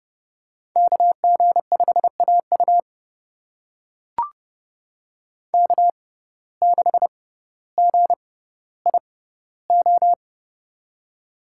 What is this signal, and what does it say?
Morse code, heard as a keyed tone, about 32 wpm, spaced as for 16 wpm, 700 Hz. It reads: KG5AU K6GIO
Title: USA Call Signs - 6x Character Spacing - 30wpm